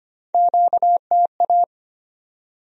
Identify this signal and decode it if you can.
Morse code, heard as a keyed tone, about 37 wpm, spaced as for 31 wpm, 700 Hz. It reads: QTA
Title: Speed-Racing Q-Codes - 25wpm